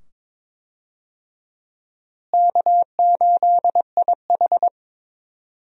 Morse code, heard as a keyed tone, about 22 wpm, 700 Hz. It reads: K8IH
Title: USA Call Signs - 4 Characters - Repeated 2x - 22wpm